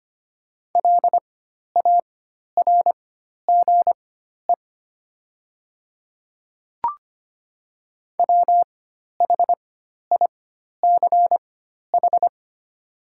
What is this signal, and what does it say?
Morse code, heard as a keyed tone, about 25 wpm, spaced as for 12 wpm, 700 Hz. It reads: LARGE WHICH
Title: 4x Character Spacing - Top 200 Words - 25wpm